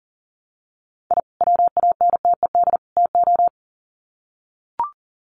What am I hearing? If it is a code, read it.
Morse code, heard as a keyed tone, about 40 wpm, 700 Hz. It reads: I WANTED TO
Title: Sets of 3 Words 40wpm